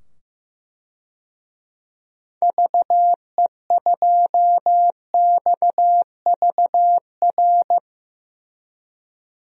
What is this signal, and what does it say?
Morse code, heard as a keyed tone, about 15 wpm, 700 Hz. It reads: VE2XVR
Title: DXCC-100 Call Signs - 6 Characters - 15wpm